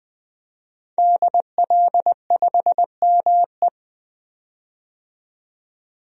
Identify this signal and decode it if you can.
Morse code, heard as a keyed tone, about 20 wpm, 700 Hz. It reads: DL5ME
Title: DXCC-100 Call Signs - 5 Characters - 20wpm